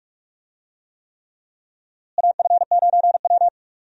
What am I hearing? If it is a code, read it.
Morse code, heard as a keyed tone, about 46 wpm, 700 Hz. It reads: AF9W
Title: USA Call Signs - 4 Characters - Repeated 3x - 45wpm